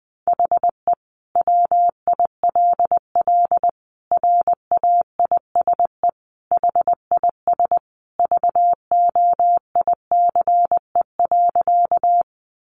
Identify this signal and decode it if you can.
Morse code, heard as a keyed tone, about 20 wpm, 700 Hz. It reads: HE WILL RAISE HIS VOICE.
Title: Sentences from Top 1400 Words 20wpm